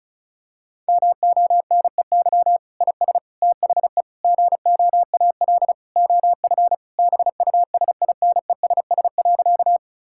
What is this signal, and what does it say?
Morse code, heard as a keyed tone, about 34 wpm, 700 Hz. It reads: MONEY IS THE GOAL OF BUSINESS.
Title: Sentences from Top 1000 Words - Review Part 5 - 35wpm